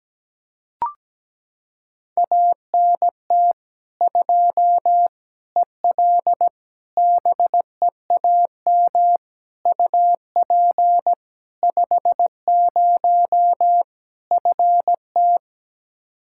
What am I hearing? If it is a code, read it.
Morse code, heard as a keyed tone, about 17 wpm, 700 Hz. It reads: ANT 2 EL BEAM UP 50 FT
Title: Practice QSO - Exchange 2 - Info - 17wpm